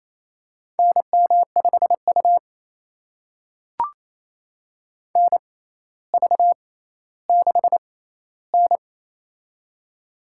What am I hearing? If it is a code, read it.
Morse code, heard as a keyed tone, about 28 wpm, spaced as for 14 wpm, 700 Hz. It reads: NM5U NV6N
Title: USA Call Signs - 4 Characters - 6x Character Spacing - 28wpm